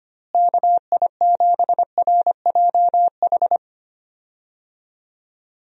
Morse code, heard as a keyed tone, about 25 wpm, 700 Hz. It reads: KI7RJH
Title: USA Call Signs - 6 Characters - Repeated 3x - 25wpm